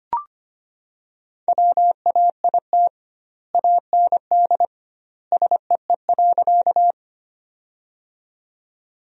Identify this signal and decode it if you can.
Morse code, heard as a keyed tone, about 25 wpm, 700 Hz. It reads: WAIT AND SEE.
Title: 2x Word-Spacing - Sentences from Top 500 Words - 25wpm